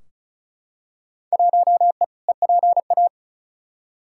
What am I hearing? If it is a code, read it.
Morse code, heard as a keyed tone, about 36 wpm, 700 Hz. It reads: 1E EPA
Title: Field Day Exchanges - 35wpm